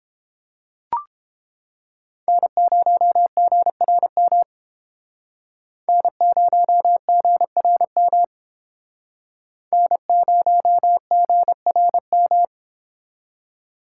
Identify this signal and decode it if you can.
Morse code, heard as a keyed tone, about 31 wpm, 700 Hz. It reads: N0GRM N0GRM N0GRM